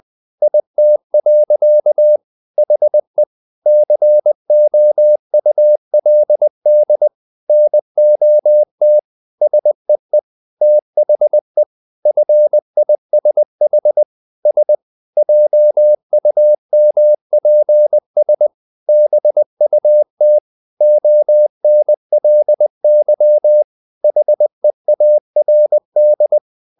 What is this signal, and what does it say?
Morse code, heard as a keyed tone, about 20 wpm, 600 Hz. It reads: IT. HE COULD NOT SEE THE FISH S JUMPS BUT ONLY HEARD
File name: the old man and the sea part 2 20wpm